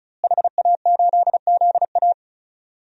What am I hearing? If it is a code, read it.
Morse code, heard as a keyed tone, about 36 wpm, 700 Hz. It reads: HA8ZA
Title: DXCC-100 Call Signs - 5 Characters - Repeated 3x - 35wpm